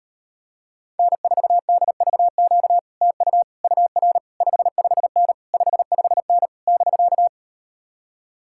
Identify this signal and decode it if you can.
Morse code, heard as a keyed tone, about 38 wpm, 700 Hz. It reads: N4DVQ TU UR 55N 55N <BK>